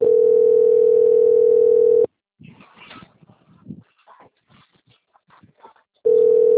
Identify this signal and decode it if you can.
Ringback tone (North America), 440 + 480 Hz.
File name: out- - - - - .32076